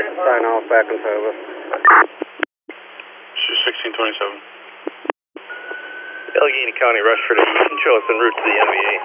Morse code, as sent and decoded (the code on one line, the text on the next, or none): none